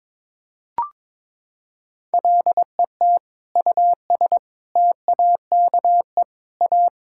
Morse code, heard as a keyed tone, about 22 wpm, 700 Hz. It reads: LET US TAKE A
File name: Sentences with 7-Word Phrases - 22wpm